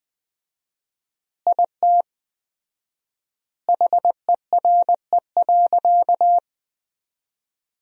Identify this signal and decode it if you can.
Morse code, heard as a keyed tone, about 19 wpm, 700 Hz. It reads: IT HERE.
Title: 4x Word-Spacing - Sentences from Top 500 Words - 20wpm